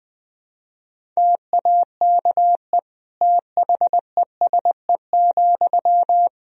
Morse code, heard as a keyed tone, about 20 wpm, 700 Hz. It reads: TAKE THESE,
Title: Sentences from Top 300 Words and Abbreviations 20wpm